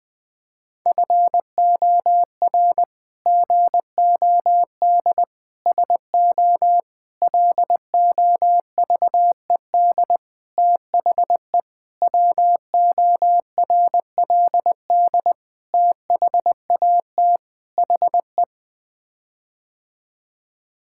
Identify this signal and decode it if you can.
Morse code, heard as a keyed tone, about 20 wpm, 700 Hz. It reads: FOR GOD SO LOVED THE WORLD THAT HE
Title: Sets of 8 Words - 20wpm